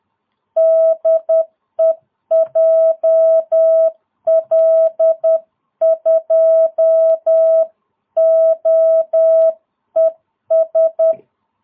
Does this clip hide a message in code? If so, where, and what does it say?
Yes: Morse code from 0.6 to 11.1 s, heard as a keyed tone, about 10 wpm, 660 Hz. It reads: DE JL2OES